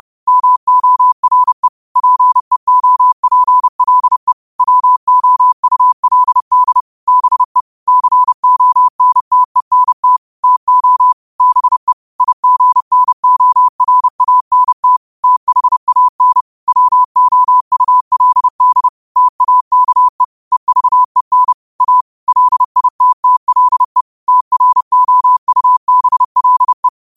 Morse code, heard as a keyed tone, about 30 wpm, 1000 Hz. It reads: MORE PEOPLE WOULD BE CONTENT TO BE IGNORANT THAN WOULD TAKE EVEN A LITTLE TROUBLE